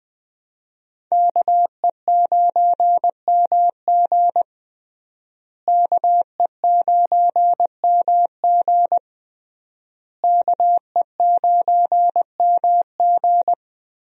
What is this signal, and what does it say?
Morse code, heard as a keyed tone, about 20 wpm, 700 Hz. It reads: KE9MG KE9MG KE9MG
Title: USA Call Signs - 5 Characters - Repeated 3x - 20wpm